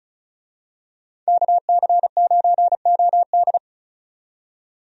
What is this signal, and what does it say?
Morse code, heard as a keyed tone, about 36 wpm, 700 Hz. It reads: KC9OD
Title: USA Call Signs - 5 Characters - 35wpm